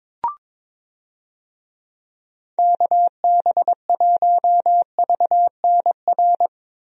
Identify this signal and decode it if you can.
Morse code, heard as a keyed tone, about 22 wpm, 700 Hz. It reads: KB1VNR
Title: USA Call Signs - 6 Characters - Repeated 3x - 22wpm